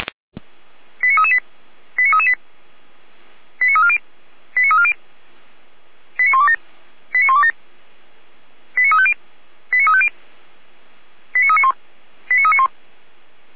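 Five-tone selective call (ZVEI), heard as quick runs of five tones, tones of about 80 ms. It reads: 89338 89338 89340 89340 89127 89127 89350 89350 89391 89391
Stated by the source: Five-tone ZVEI selective call 89303 received by SDR.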